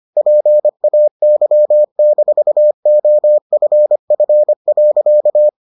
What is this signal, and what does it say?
Morse code, heard as a keyed tone, about 25 wpm, 600 Hz. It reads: PAY-OFF.